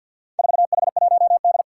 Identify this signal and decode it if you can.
Morse code, heard as a keyed tone, about 52 wpm, 700 Hz. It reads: 4H1D
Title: Four Challenging Letter-Number Combinations 50wpm